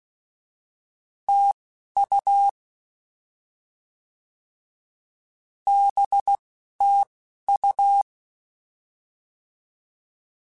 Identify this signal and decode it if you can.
Morse code, heard as a keyed tone, about 16 wpm, spaced as for 11 wpm, 800 Hz. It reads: TU BTU
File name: Misc / Misc-mp3s / Words @10